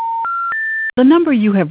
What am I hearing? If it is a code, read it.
Special information tone, with speech.